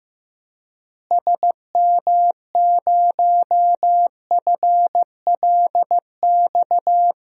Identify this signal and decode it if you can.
Morse code, heard as a keyed tone, about 15 wpm, 700 Hz. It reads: SM0FLX
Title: DXCC-100 Call Signs - 6 Characters - Repeated 3x - 15wpm